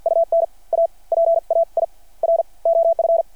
Morse code, heard as a keyed tone, about 53 wpm, spaced as for 35 wpm, 690 Hz. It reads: UN A PAIR OF
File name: archived / CRSnet